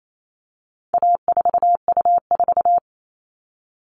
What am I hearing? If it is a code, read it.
Morse code, heard as a keyed tone, about 28 wpm, 700 Hz. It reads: A4U4
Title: All Cut Numbers - 4-Digit 28wpm